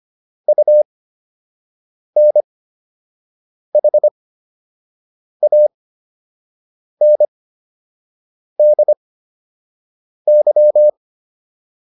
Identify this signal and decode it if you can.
Morse code, heard as a keyed tone, about 25 wpm, spaced as for 6 wpm, 600 Hz. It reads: UNHANDY